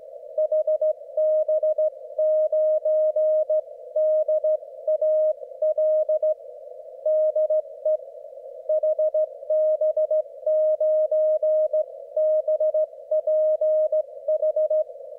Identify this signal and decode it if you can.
Morse code, heard as a keyed tone, about 15 wpm, 620 Hz. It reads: HB9DAL DE HB9BPH
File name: Morsekurs / Witze / QSO W4 HB9DAL und HB9BPH